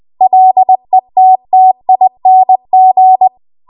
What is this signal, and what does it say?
Morse code, heard as a keyed tone, about 20 wpm, 750 Hz. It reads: LETTING
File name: Morse / 20WPM